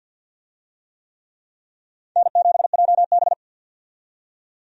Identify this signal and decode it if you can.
Morse code, heard as a keyed tone, about 47 wpm, 700 Hz. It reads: N7JB
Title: USA Call Signs - 4 Characters - 50wpm